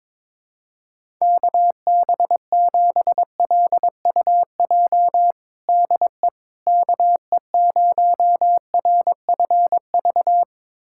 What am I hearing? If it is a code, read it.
Morse code, heard as a keyed tone, about 22 wpm, 700 Hz. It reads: KB7LUJ DE KE0RFV
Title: USA Callsign DE Callsign - 6 Characters - 22wpm